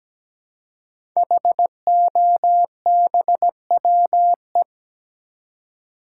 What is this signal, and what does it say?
Morse code, heard as a keyed tone, about 17 wpm, 700 Hz. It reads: HOBWE
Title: Code Groups - 5 Letters - 17wpm